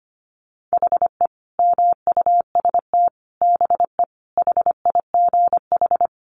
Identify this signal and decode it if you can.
Morse code, heard as a keyed tone, about 25 wpm, 700 Hz. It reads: HE MUST BE HIGH
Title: Sentences from Top 1200 Words 25wpm